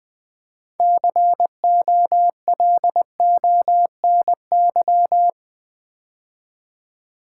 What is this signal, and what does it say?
Morse code, heard as a keyed tone, about 20 wpm, 690 Hz. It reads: COLONY
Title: Build A Word - 3 Letters At A Time - 6-Letter Words - 20wpm